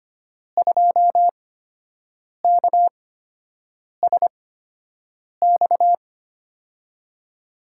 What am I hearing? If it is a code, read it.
Morse code, heard as a keyed tone, about 25 wpm, spaced as for 7 wpm, 700 Hz. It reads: 2KSX
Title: USA Call Signs - 6 Characters - 8x Character Spacing - 25wpm